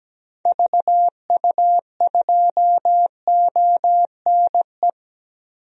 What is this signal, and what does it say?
Morse code, heard as a keyed tone, about 17 wpm, 700 Hz. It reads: VU2ONE
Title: DXCC-100 Call Signs - 6 Characters - 17wpm